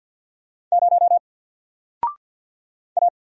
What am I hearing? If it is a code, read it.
Morse code, heard as a keyed tone, about 50 wpm, 700 Hz. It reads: 0 A
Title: Single Letter-Number Alt 50wpm